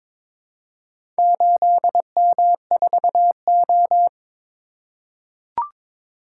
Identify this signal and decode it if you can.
Morse code, heard as a keyed tone, about 22 wpm, 700 Hz. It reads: 8M4O
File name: practice / 22wpm / Four Letter-Number Combinations - 22wpm